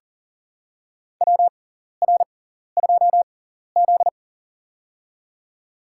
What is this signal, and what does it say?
Morse code, heard as a keyed tone, about 38 wpm, spaced as for 14 wpm, 700 Hz. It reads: WR2Z
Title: USA Call Signs - 4 Characters - 6x Character Spacing - 40wpm